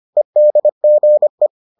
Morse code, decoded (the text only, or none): EDGE